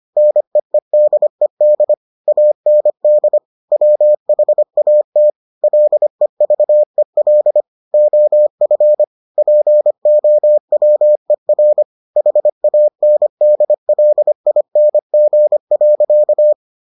Morse code, heard as a keyed tone, about 25 wpm, 600 Hz. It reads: NEEDED AND WHAT LEVEL OF POWER HANDLING.